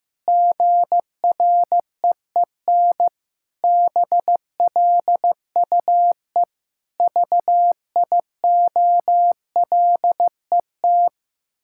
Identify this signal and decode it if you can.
Morse code, heard as a keyed tone, about 15 wpm, 700 Hz. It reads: GREEN BLUE VIOLET
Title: Pangram Challenge v2 - 15wpm